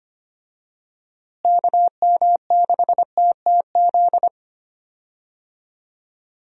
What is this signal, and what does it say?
Morse code, heard as a keyed tone, about 25 wpm, 700 Hz. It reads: KM6TTZ
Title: USA Call Signs - 6 Characters - Repeated 2x - 25wpm